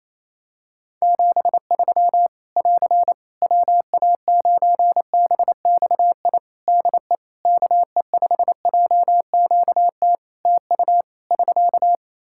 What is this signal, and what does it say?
Morse code, heard as a keyed tone, about 28 wpm, 700 Hz. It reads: 73 <AR> WA9BXI DE KE5JQT TU <SK>
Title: Practice QSO - Exchange 4 - Ending - 28wpm